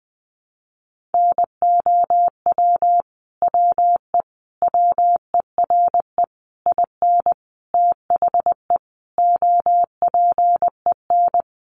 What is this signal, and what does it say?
Morse code, heard as a keyed tone, about 20 wpm, 700 Hz. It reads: NOW WE WERE IN THE OPEN
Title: Sentences from Top 1200 Words 20wpm